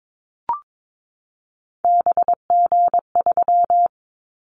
Morse code, heard as a keyed tone, about 22 wpm, 700 Hz. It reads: BG3
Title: Callsign Prefixes for Top 100 DXCC - 22wpm